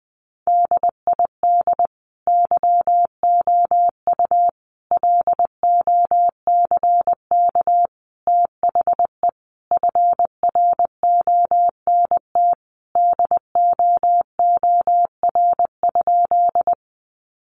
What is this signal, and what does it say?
Morse code, heard as a keyed tone, about 20 wpm, 700 Hz. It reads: DID YOU LOCK THE FRONT DOOR?
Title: Sentences from Top 1300 Words 20wpm